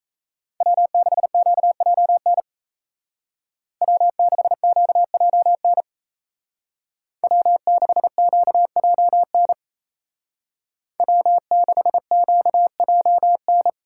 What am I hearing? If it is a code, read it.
Morse code, heard as a keyed tone, about 34 wpm, 700 Hz. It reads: W6QJN W6QJN W6QJN W6QJN
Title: Speed Racing - USA 1x3 Call Signs 28wpm